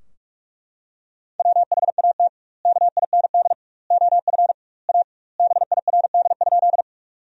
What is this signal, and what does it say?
Morse code, heard as a keyed tone, about 42 wpm, 700 Hz. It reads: WHAT KIND OF A BIRD?
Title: 2x Word-Spacing - Sentences from Top 400 Words - 45wpm